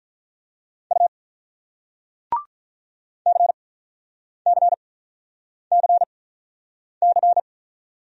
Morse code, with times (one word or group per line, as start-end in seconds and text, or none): 0.9-1.1 U
3.3-7.4 CCCC